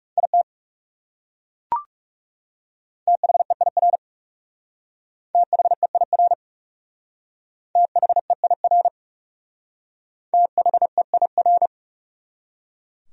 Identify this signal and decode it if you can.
Morse code, heard as a keyed tone, about 37 wpm, 700 Hz. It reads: IT THEIR THEIR THEIR THEIR
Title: Speed-Racing Top 300 Words 30wpm